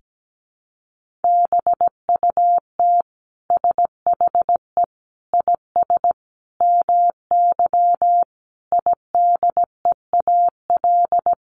Morse code, heard as a keyed tone, about 17 wpm, 700 Hz. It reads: BUT SHE IS MY IDEAL
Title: Sentences from Top 1400 Words 17wpm